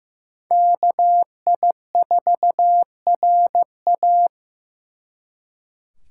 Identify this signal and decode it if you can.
Morse code, heard as a keyed tone, about 18 wpm, 700 Hz. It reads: KI4RA
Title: Speed Racing - USA 2x2 Call Signs 15wpm